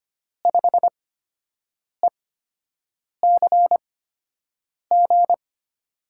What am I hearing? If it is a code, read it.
Morse code, heard as a keyed tone, about 25 wpm, spaced as for 7 wpm, 700 Hz. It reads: 5ECG
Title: USA Call Signs - 5 Characters - 8x Character Spacing - 25wpm